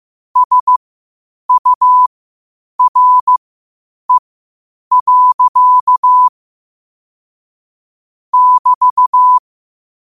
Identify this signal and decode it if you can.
Morse code, heard as a keyed tone, about 15 wpm, spaced as for 8 wpm, 1000 Hz. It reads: SURE. <BT>
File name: qotdfiles / 8wpm